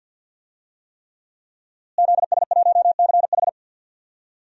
Morse code, heard as a keyed tone, about 53 wpm, 700 Hz. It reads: ZS1CH
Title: DXCC-100 Call Signs - 5 Characters - Repeated 3x - 50wpm